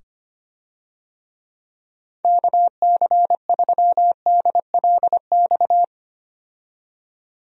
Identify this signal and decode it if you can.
Morse code, heard as a keyed tone, about 25 wpm, 700 Hz. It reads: KC3DLX
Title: USA Call Signs - 6 Characters - 25wpm